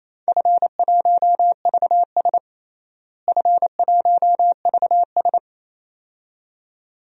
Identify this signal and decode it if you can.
Morse code, heard as a keyed tone, about 27 wpm, 700 Hz. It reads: F1VS F1VS
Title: DXCC-100 Call Signs - 4 Characters - Repeated 2x - 28wpm